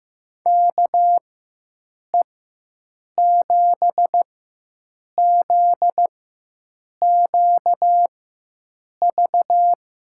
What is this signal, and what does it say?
Morse code, heard as a keyed tone, about 16 wpm, spaced as for 7 wpm, 700 Hz. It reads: KE7ZQV
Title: USA Call Signs - 6 Characters - 4x Character Spacing - 15wpm